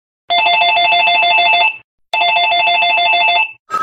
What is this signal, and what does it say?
There is no code.